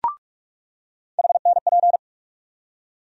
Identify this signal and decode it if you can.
Morse code, heard as a keyed tone, about 42 wpm, 700 Hz. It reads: HNP
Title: USA Call Sign Postfixes - 3 Letters 45wpm